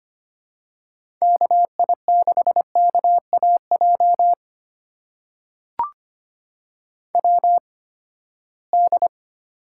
Morse code, heard as a keyed tone, about 25 wpm, spaced as for 17 wpm, 700 Hz. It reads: KI6KAJ WD
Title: USA Call Signs - 6 Characters - 8x Character Spacing - 25wpm